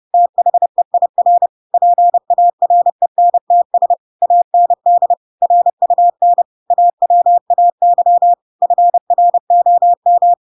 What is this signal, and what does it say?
Morse code, heard as a keyed tone, about 30 wpm, 700 Hz. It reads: THEIR PARENTS AND RUN AWAY FROM
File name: Pinocchio / Pinocchio ch4 30 30 ik1hge